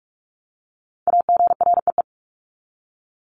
Morse code, heard as a keyed tone, about 44 wpm, 700 Hz. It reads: AGREE
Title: Top 1001-1250 Words 45wpm